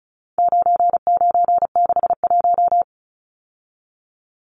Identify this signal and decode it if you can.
Morse code, heard as a keyed tone, about 34 wpm, 700 Hz. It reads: 9961
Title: Numbers - 4-Digit 35wpm